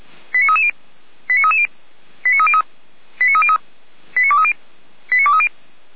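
Five-tone selective call (ZVEI), heard as quick runs of five tones, tones of about 70 ms. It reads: 89330 89330 89303 89303 89230 89230
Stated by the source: ZVEI five-tone selective call 89319 received on SDR, fire brigade Feuerwehr Orlamünde.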